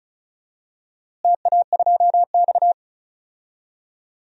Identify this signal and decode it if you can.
Morse code, heard as a keyed tone, about 36 wpm, 700 Hz. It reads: TA2X
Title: DXCC-100 Call Signs - 4 Characters - Repeated 3x - 35wpm